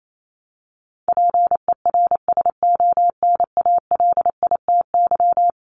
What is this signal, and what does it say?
Morse code, heard as a keyed tone, about 28 wpm, 700 Hz. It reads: PERSONALITY
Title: Top 1251-1500 Words 28wpm